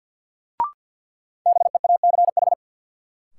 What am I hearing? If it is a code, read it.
Morse code, heard as a keyed tone, about 47 wpm, 700 Hz. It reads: BEACH
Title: Sentences from Top 1500 Words - New Word Drill - 50wpm